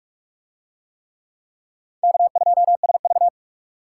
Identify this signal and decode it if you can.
Morse code, heard as a keyed tone, about 46 wpm, 700 Hz. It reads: K2SV